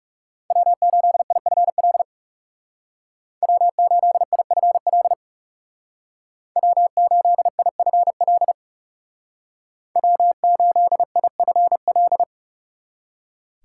Morse code, heard as a keyed tone, about 35 wpm, 700 Hz. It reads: W8IFL W8IFL W8IFL W8IFL
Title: Speed Racing - USA 1x3 Call Signs 30wpm